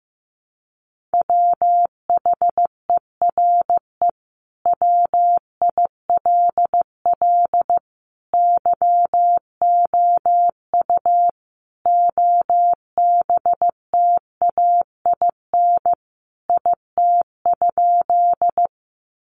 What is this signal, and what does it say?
Morse code, heard as a keyed tone, about 15 wpm, 700 Hz. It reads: WHERE WILL YOU OBTAIN IT?
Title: Sentences from Top 1300 Words 15wpm